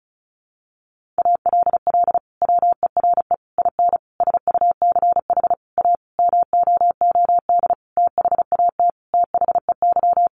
Morse code, heard as a keyed tone, about 34 wpm, 700 Hz. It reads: ALL WERE IN SUCH A MOOD THAT THEY
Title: Sentences from Top 1100 Words 35wpm